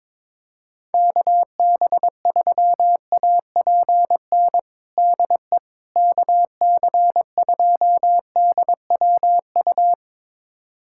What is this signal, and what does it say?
Morse code, heard as a keyed tone, about 22 wpm, 700 Hz. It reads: KB3APN DE KC2DWU
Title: USA Callsign DE Callsign - 6 Characters - 22wpm